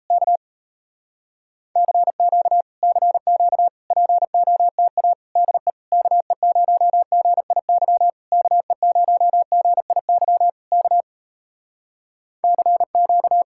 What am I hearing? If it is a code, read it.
Morse code, heard as a keyed tone, about 33 wpm, 700 Hz. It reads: K CQ CQ POTA DE KE0GIY KE0GIY K CQ
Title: Speed-Racing Common QSO Phrases Encore - 28wpm